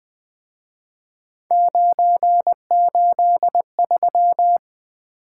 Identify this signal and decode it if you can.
Morse code, heard as a keyed tone, about 23 wpm, 700 Hz. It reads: 983
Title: Speed-Racing - Three Letter-Number Combinations - 20wpm